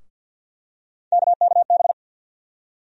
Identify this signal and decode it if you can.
Morse code, heard as a keyed tone, about 48 wpm, 700 Hz. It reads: KKB
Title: USA Call Sign Postfixes - 3 Letters 50wpm